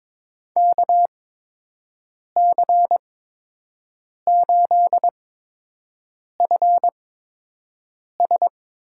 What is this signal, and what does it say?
Morse code, heard as a keyed tone, about 21 wpm, spaced as for 6 wpm, 700 Hz. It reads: KC8FS
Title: USA Call Signs - 8x Character Spacing - 22wpm